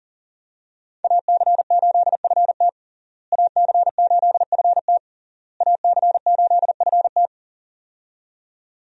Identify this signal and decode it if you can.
Morse code, heard as a keyed tone, about 40 wpm, 700 Hz. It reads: AC8FT AC8FT AC8FT
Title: USA Call Signs - 5 Characters - Repeated 3x - 40wpm